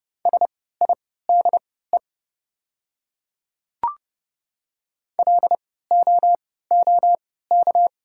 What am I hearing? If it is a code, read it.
Morse code, heard as a keyed tone, about 29 wpm, spaced as for 17 wpm, 690 Hz. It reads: SIDE LOOK
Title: 3x Character Spacing - Top 100 Words - 30wpm